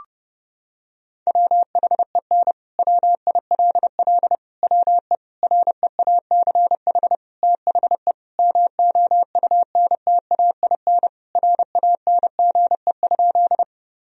Morse code, heard as a keyed tone, about 30 wpm, 700 Hz. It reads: WHEN WILL WE REACH THE MOUNTAIN RANGE?
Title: Sentences from Top 1000 Words - Review Part 4 - 30wpm